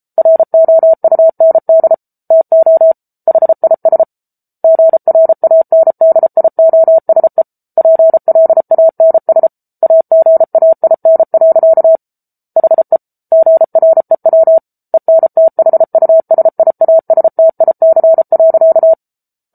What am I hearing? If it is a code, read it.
Morse code, heard as a keyed tone, about 34 wpm, 650 Hz. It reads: ROUND TO HIS GRANDIOSE PLANS AGAIN. HE GREW ENTHUSIASTIC.
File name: Book Two / 07-Chapter 7 - 33 WPM